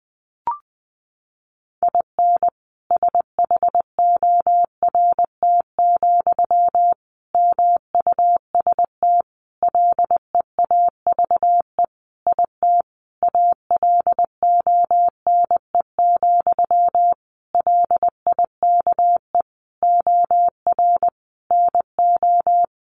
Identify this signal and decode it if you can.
Morse code, heard as a keyed tone, about 20 wpm, 700 Hz. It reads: IN SHORT, MUST LEAVE IT ALONE, LIKE OR NO
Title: Sentences from Top 1000 Words 20wpm